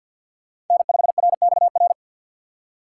Morse code, heard as a keyed tone, about 52 wpm, 700 Hz. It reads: N5RXR
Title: USA Call Signs - 5 Characters - 6x Character Spacing - 50wpm